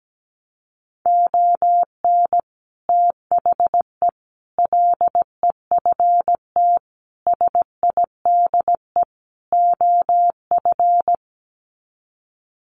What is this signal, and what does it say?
Morse code, heard as a keyed tone, about 17 wpm, 700 Hz. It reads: ON THE LEFT SIDE OF
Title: Sets of 5 Words Encore2 17wpm